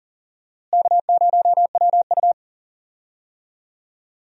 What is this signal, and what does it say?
Morse code, heard as a keyed tone, about 40 wpm, 700 Hz. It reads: K0WU
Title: USA Call Signs - 4 Characters - Repeated 2x - 40wpm